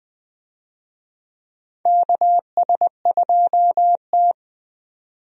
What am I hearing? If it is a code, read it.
Morse code, heard as a keyed tone, about 20 wpm, 700 Hz. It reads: KS2T
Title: USA Call Signs - 4 Characters - 20wpm